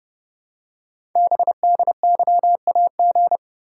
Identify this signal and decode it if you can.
Morse code, heard as a keyed tone, about 30 wpm, 700 Hz. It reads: BDYAG
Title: Code Groups - 5 Letters - 30wpm